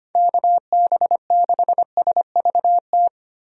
Morse code, heard as a keyed tone, about 25 wpm, 700 Hz. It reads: KB6SVT
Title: USA Call Signs - 6 Characters - 25wpm